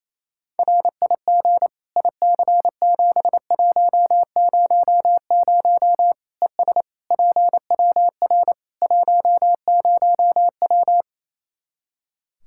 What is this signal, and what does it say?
Morse code, heard as a keyed tone, about 28 wpm, 700 Hz. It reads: RIG IC7100 ES PWR 10W
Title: K1USN SST QSO Practice 28wpm